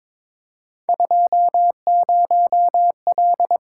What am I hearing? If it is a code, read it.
Morse code, heard as a keyed tone, about 22 wpm, 700 Hz. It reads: 20L
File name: practice / 22wpm / Three Letter-Number Combinations - 22wpm